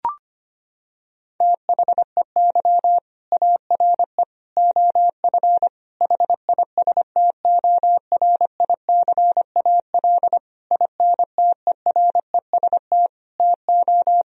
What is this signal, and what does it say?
Morse code, heard as a keyed tone, about 25 wpm, 700 Hz. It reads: THEY ARE OF HISTORICAL INTEREST TO
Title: Sentences from Top 1000 Words - Review Part 4 - 25wpm